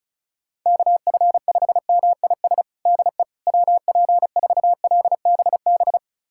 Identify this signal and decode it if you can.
Morse code, heard as a keyed tone, about 34 wpm, 690 Hz. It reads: KF5MIS DE WP4LBB
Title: USA Callsign DE Callsign - 6 Characters - 35wpm